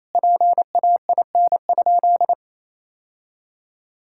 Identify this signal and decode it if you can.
Morse code, heard as a keyed tone, about 28 wpm, 700 Hz. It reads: PAIN?
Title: Sentences from Top 1600 Words 28wpm